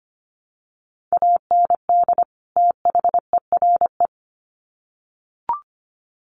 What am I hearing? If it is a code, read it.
Morse code, heard as a keyed tone, about 25 wpm, 700 Hz. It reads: AND THERE
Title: Sets of 2 Words 25wpm